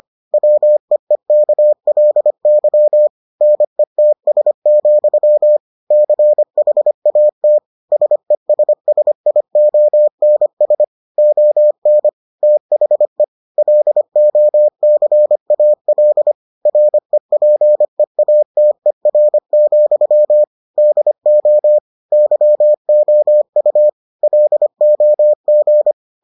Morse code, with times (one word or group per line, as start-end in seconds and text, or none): 0.3-3.1 WEEKLY
3.4-5.6 NETS,
5.9-7.6 CHAT
7.9-10.8 SESSIONS
11.2-12.1 ON
12.4-13.2 THE
13.6-16.3 LOCAL
16.7-20.4 REPEATER,
20.8-21.8 DO
22.1-23.9 YOU
24.2-25.9 LOG